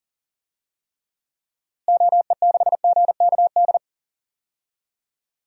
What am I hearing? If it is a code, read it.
Morse code, heard as a keyed tone, about 39 wpm, 700 Hz. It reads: OE6GKD
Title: DXCC-100 Call Signs - 6 Characters - Repeated 2x - 40wpm